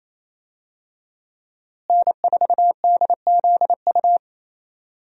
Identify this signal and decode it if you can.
Morse code, heard as a keyed tone, about 28 wpm, 700 Hz. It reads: N4DZU
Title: USA Call Signs - 5 Characters - Repeated 2x - 28wpm